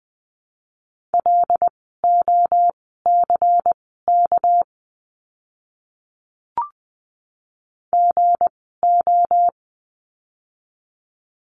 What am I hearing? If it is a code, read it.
Morse code, heard as a keyed tone, about 19 wpm, spaced as for 14 wpm, 700 Hz. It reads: LOCK GO
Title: 199 Lesson 32f - Letter k - Words 20wpm